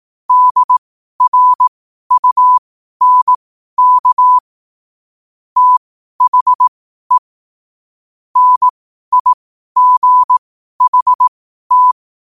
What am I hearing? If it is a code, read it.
Morse code, heard as a keyed tone, about 18 wpm, spaced as for 12 wpm, 1000 Hz. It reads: DRUNK THE NIGHT